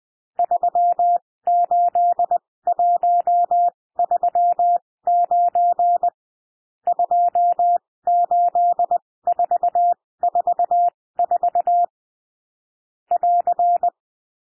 Morse code, heard as a keyed tone, about 20 wpm, spaced as for 16 wpm, 700 Hz. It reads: 38139 28444 <AR>
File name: cwmp3 / N1 SOUND NUMBERS 20 wpm